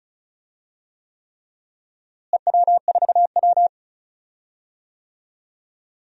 Morse code, heard as a keyed tone, about 36 wpm, 700 Hz. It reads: EW4W
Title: DXCC-100 Call Signs - 4 Characters - Repeated 3x - 35wpm